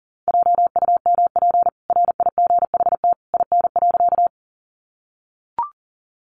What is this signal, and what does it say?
Morse code, heard as a keyed tone, about 40 wpm, 700 Hz. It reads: JUMP RIGHT IN.
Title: Sentences from Top 1300 Words 40wpm